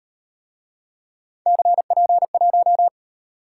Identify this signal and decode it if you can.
Morse code, heard as a keyed tone, about 38 wpm, 700 Hz. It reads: CP1